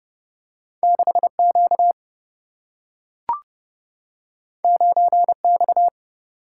Morse code, heard as a keyed tone, about 30 wpm, 700 Hz. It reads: 6Q 9X
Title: Single-Letter Single-Number 30wpm